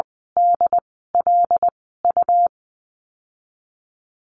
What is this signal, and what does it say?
Morse code, heard as a keyed tone, about 20 wpm, spaced as for 14 wpm, 700 Hz. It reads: DLU
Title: 140 Lesson 23g - Letter f - Call Signs 20wpm